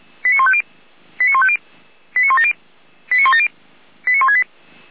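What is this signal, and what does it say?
Five-tone selective call (ZVEI), heard as quick runs of five tones, tones of about 70 ms. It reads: 89150 89150 89170 89170 89168
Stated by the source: ZVEI five-tone selective call received by SDR.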